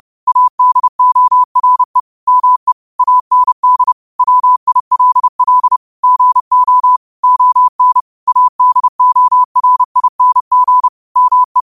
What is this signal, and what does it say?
Morse code, heard as a keyed tone, about 30 wpm, 1000 Hz. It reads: ADORE ME AND WILL GO ON ADORING ME
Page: Quotes for Fri, 15 Aug 2025 in Morse Code at 30 words per minute.